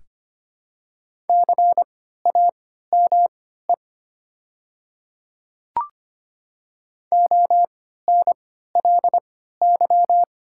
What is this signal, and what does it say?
Morse code, heard as a keyed tone, about 25 wpm, spaced as for 14 wpm, 690 Hz. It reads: CAME ONLY
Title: 3x Character Spacing - Top 200 Words - 25wpm